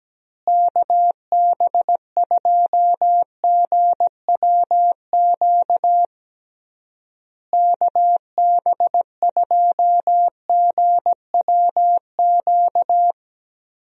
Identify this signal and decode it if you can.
Morse code, heard as a keyed tone, about 17 wpm, 700 Hz. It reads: KB2GWQ KB2GWQ
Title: USA Call Signs - 6 Characters - Repeated 3x - 17wpm